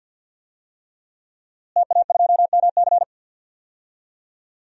Morse code, heard as a keyed tone, about 51 wpm, 700 Hz. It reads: TA2MC
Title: DXCC-100 Call Signs - 5 Characters - Repeated 3x - 50wpm